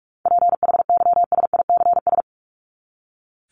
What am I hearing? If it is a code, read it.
Morse code, heard as a keyed tone, about 44 wpm, 700 Hz. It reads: PHYSICS
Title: Top 501-750 Words 45wpm